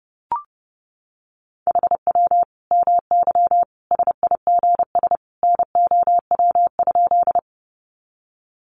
Morse code, heard as a keyed tone, about 30 wpm, 700 Hz. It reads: HW MY SIGS NOW?
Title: Common QSO Phrases 30wpm